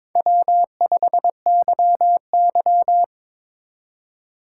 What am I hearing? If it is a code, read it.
Morse code, heard as a keyed tone, about 22 wpm, 700 Hz. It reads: W5YY
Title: Speed Racing - USA 1x2 Call Signs 22wpm